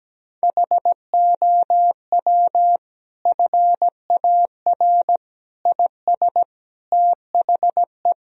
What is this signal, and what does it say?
Morse code, heard as a keyed tone, about 17 wpm, 700 Hz. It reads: HOW FAR IS THE
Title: Sentences from Top 1000 Words - Review Part 4 - 17wpm